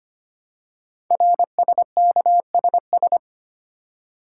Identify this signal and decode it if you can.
Morse code, heard as a keyed tone, about 25 wpm, 700 Hz. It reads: RSKSS
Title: Code Groups - 5 Letters - 25wpm